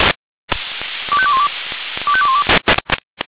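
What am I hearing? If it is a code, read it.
Five-tone selective call (ZVEI), heard as quick runs of five tones, tones of about 70 ms. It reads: 25212 25212
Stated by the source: Received on SDR, ZVEI five-tone selective call 25355.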